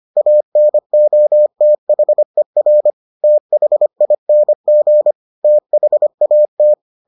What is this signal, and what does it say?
Morse code, heard as a keyed tone, about 25 wpm, 600 Hz. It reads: ANOTHER THING THAT